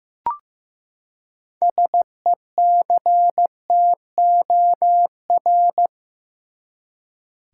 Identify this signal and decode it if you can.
Morse code, heard as a keyed tone, about 15 wpm, 700 Hz. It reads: SECTOR
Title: Six Letter Words - 15wpm